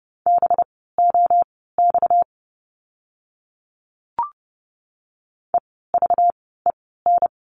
Morse code, heard as a keyed tone, about 30 wpm, spaced as for 17 wpm, 700 Hz. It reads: BOX EVEN
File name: course / 30:10wpm / 230 Lesson 37f - Letter x - Words 30wpm